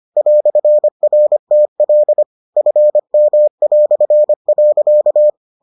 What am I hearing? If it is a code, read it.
Morse code, heard as a keyed tone, about 25 wpm, 600 Hz. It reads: "RTL FM".